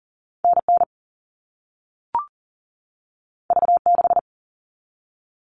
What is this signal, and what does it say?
Morse code, heard as a keyed tone, about 38 wpm, 700 Hz. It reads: NN V6
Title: All Cut Numbers - 2-Digit 40wpm